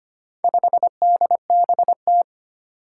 Morse code, heard as a keyed tone, about 25 wpm, 700 Hz. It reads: 5DBT
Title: Four Challenging Letter-Number Combinations 25wpm